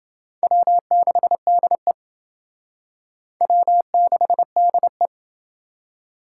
Morse code, heard as a keyed tone, about 29 wpm, 700 Hz. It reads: W6DE W6DE